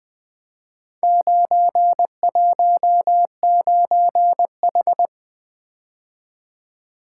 Morse code, heard as a keyed tone, about 20 wpm, 700 Hz. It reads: 919H
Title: Four Letter-Number Combinations - 20wpm